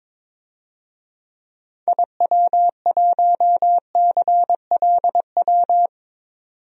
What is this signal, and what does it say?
Morse code, heard as a keyed tone, about 22 wpm, 700 Hz. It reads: IW1CLW
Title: DXCC-100 Call Signs - 6 Characters - 22wpm